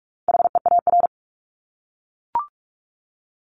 Morse code, heard as a keyed tone, about 44 wpm, 700 Hz. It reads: HEAR
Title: ING Suffix - Top 100 Words with Base Words 45wpm